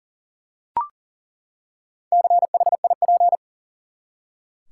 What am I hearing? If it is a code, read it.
Morse code, heard as a keyed tone, about 40 wpm, 700 Hz. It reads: CHIP
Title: Four Letter Words - 40wpm